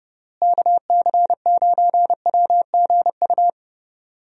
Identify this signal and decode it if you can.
Morse code, heard as a keyed tone, about 30 wpm, 700 Hz. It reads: KC9WGU
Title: USA Call Signs - 6 Characters - Repeated 3x - 30wpm